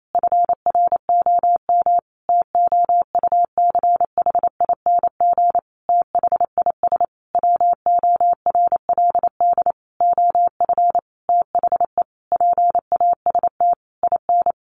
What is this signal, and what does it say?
Morse code, heard as a keyed tone, about 28 wpm, 700 Hz. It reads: FROM TOUCHING THIS WORLD OF THE PAST IN
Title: A Sound Of Thunder 28wpm